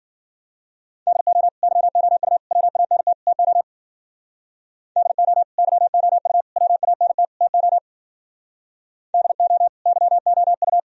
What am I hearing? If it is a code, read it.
Morse code, heard as a keyed tone, about 53 wpm, 700 Hz. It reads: DO YOU WANT TO DO YOU WANT TO DO YOU